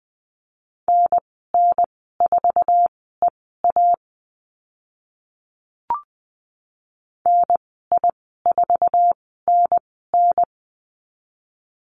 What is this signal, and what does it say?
Morse code, heard as a keyed tone, about 20 wpm, spaced as for 14 wpm, 700 Hz. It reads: NN4EA NI4NN
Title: 076 Lesson 13g - Letter L - Call Signs 20wpm